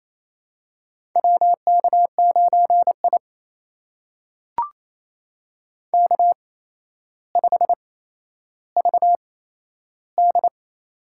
Morse code, heard as a keyed tone, about 27 wpm, spaced as for 12 wpm, 700 Hz. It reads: WK9I K5VD
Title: USA Call Signs - 4 Characters - 8x Character Spacing - 28wpm